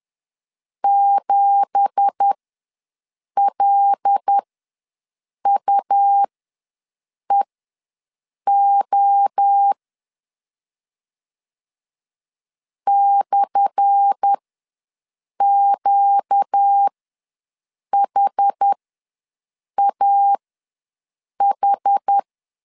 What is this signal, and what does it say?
Morse code, heard as a keyed tone, about 10 wpm, spaced as for 6 wpm, 800 Hz. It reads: 7LUEO /QHAH
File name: LEZIONI / 10 SIGNS / 10D COMA ws 10wpm 4314